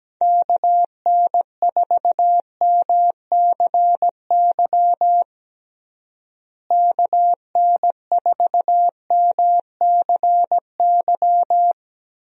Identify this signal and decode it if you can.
Morse code, heard as a keyed tone, about 17 wpm, 700 Hz. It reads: KN4MCY KN4MCY
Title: USA Call Signs - 6 Characters - Repeated 2x - 17wpm